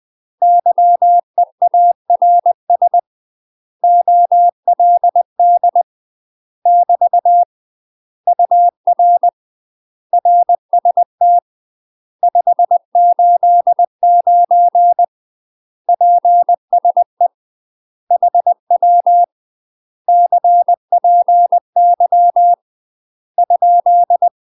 Morse code, heard as a keyed tone, about 20 wpm, 700 Hz. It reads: YEARS OLD <BT> UR RST 589 PSE HW CPY ?
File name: UK MASTER QSOs TRAINING / 20WPM / UK MASTER QSOs 0037